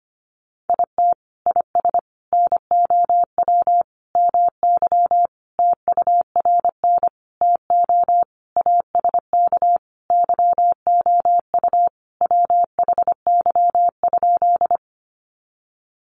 Morse code, heard as a keyed tone, about 25 wpm, 700 Hz. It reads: IT IS NOW MY TURN TO ASK YOU WHY?
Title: Sentences from Top 200 Words 25wpm